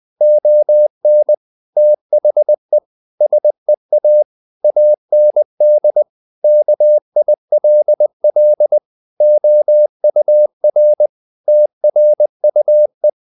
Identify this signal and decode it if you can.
Morse code, heard as a keyed tone, about 20 wpm, 600 Hz. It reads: ON THE SEA AND KILL OUR TRUE
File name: the old man and the sea part 2 20wpm